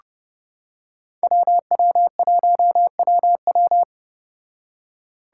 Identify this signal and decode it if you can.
Morse code, heard as a keyed tone, about 29 wpm, 700 Hz. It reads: WW1WW